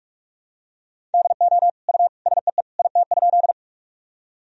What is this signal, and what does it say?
Morse code, heard as a keyed tone, about 46 wpm, 700 Hz. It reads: DO U SEE IT?
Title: Sentences from Top 100 Words and Abbreviations 45wpm